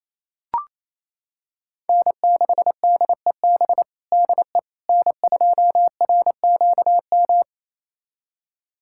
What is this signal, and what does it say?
Morse code, heard as a keyed tone, about 27 wpm, 700 Hz. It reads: N6DEB DE N2RQM
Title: USA Callsign DE Callsign - 5 Characters - 28wpm